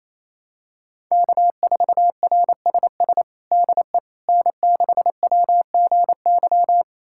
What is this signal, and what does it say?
Morse code, heard as a keyed tone, about 28 wpm, 700 Hz. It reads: K4RSS DE N6WGY
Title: USA Callsign DE Callsign 28wpm